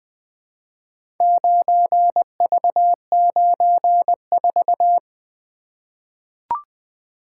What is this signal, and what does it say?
Morse code, heard as a keyed tone, about 20 wpm, 700 Hz. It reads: 9V94
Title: Four Letter-Number Combinations - 20wpm